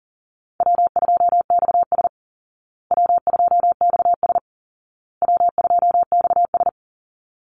Morse code, heard as a keyed tone, about 40 wpm, 700 Hz. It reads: W2XS W2XS W2XS
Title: Easy Call Signs Repeated 3x - 40wpm